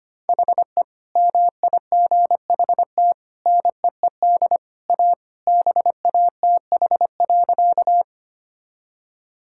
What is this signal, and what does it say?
Morse code, heard as a keyed tone, about 25 wpm, 700 Hz. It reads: HE MIGHT NEED A BATH.
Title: Sentences from Top 1600 Words 25wpm